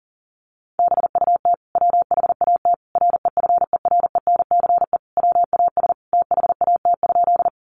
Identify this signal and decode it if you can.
Morse code, heard as a keyed tone, about 40 wpm, 700 Hz. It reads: BUT WHAT REFERENCE WAS THAT?
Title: Sentences from Top 1200 Words 40wpm